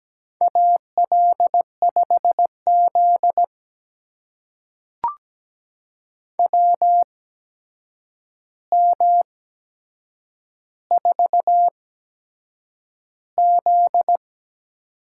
Morse code, heard as a keyed tone, about 17 wpm, spaced as for 7 wpm, 700 Hz. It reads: AL5Z WM4Z
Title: USA Call Signs - 4 Characters - 8x Character Spacing - 17wpm